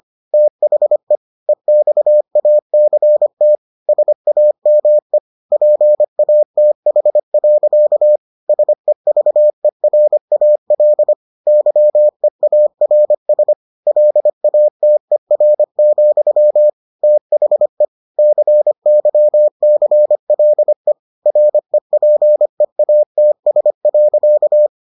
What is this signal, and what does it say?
Morse code, heard as a keyed tone, about 25 wpm, 600 Hz. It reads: THE EXACT SAME PATH. SEVERAL YEARS LATER, THE CYCLE REPEATS.